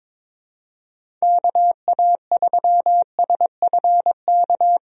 Morse code, heard as a keyed tone, about 22 wpm, 700 Hz. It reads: KA3SFK
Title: USA Call Signs - 6 Characters - 22wpm